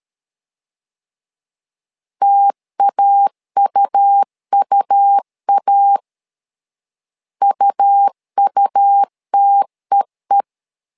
Morse code, heard as a keyed tone, about 13 wpm, 800 Hz. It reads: TAUUA UUTEE
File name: LEZIONI / 2 UHDN / 2 U 12wpm 1338